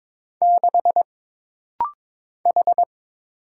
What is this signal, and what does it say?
Morse code, heard as a keyed tone, about 22 wpm, 700 Hz. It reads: 6 H
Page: This rapid-fire timing is geared to push your ICR (Instant Character Recognition) skills.